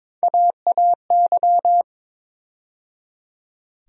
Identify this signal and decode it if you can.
Morse code, heard as a keyed tone, about 21 wpm, 700 Hz. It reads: AAY
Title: USA Call Sign Postfixes - 3 Letters 22wpm